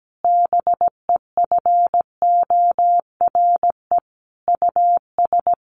Morse code, heard as a keyed tone, about 17 wpm, 700 Hz. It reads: BEFORE US
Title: Sentences from Top 1400 Words 17wpm